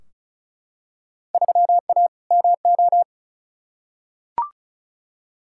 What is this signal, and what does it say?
Morse code, heard as a keyed tone, about 36 wpm, 700 Hz. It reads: 3A MO
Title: Field Day Exchanges Encore - 35wpm